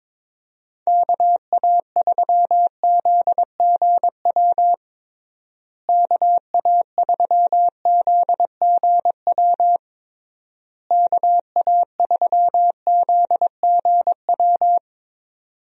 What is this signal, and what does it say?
Morse code, heard as a keyed tone, about 22 wpm, 700 Hz. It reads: KA3ZGW KA3ZGW KA3ZGW
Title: USA Call Signs - 6 Characters - Repeated 3x - 22wpm